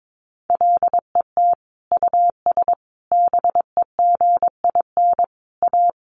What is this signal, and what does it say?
Morse code, heard as a keyed tone, about 22 wpm, 700 Hz. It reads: LET US BEGIN A
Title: Sentences from Top 1400 Words 22wpm